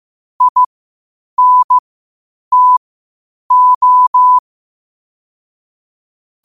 Morse code, heard as a keyed tone, about 15 wpm, spaced as for 8 wpm, 1000 Hz. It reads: INTO
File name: qotdfiles / 8wpm